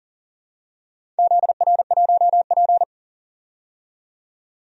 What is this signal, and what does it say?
Morse code, heard as a keyed tone, about 40 wpm, 700 Hz. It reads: ZR1P
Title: DXCC-100 Call Signs - 4 Characters - 40wpm